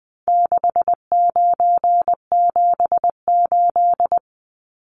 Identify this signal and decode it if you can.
Morse code, heard as a keyed tone, about 19 wpm, 700 Hz. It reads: 6978
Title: Numbers - 4-Digit 20wpm